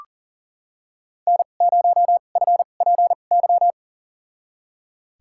Morse code, heard as a keyed tone, about 40 wpm, spaced as for 29 wpm, 700 Hz. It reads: N0FPY
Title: USA Call Signs - 5 Characters - 2x Character Spacing - 40wpm